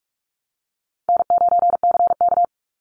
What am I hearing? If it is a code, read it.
Morse code, heard as a keyed tone, about 44 wpm, 700 Hz. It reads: N9CK
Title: Call Signs Repeated 3x - 45wpm